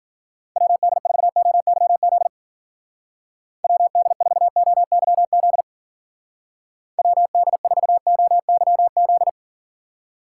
Morse code, heard as a keyed tone, about 47 wpm, 700 Hz. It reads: WD4OYZ WD4OYZ WD4OYZ
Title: Speed Racing - USA 2x3 Call Signs 40wpm